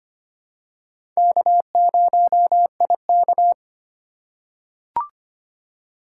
Morse code, heard as a keyed tone, about 25 wpm, 700 Hz. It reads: K0IK
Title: USA Call Signs - 3x Character Spacing - 25wpm